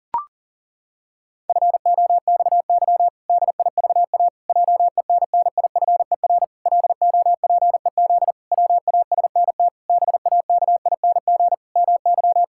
Morse code, heard as a keyed tone, about 40 wpm, 700 Hz. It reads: FOXY DIVA JENNIFER LOPEZ WASNT BAKING MY
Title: Pangram Challenge v2 - 40wpm